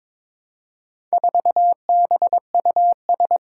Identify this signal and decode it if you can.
Morse code, heard as a keyed tone, about 22 wpm, 700 Hz. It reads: 4BUS
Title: Four Challenging Letter-Number Combinations 22wpm